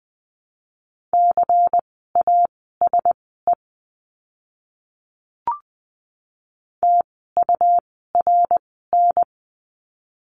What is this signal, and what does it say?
Morse code, heard as a keyed tone, about 20 wpm, spaced as for 14 wpm, 700 Hz. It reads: CASE TURN
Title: 101 Lesson 17f - Letter u - Words 20wpm